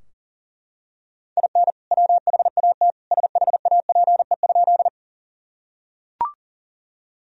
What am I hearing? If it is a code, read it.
Morse code, heard as a keyed tone, about 40 wpm, 700 Hz. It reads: IN WHAT SHAPE?
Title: Sentences from Top 1000 Words - Review Part 3 - 40wpm